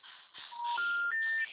Special information tone, with noise.
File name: out- -2005- - - .321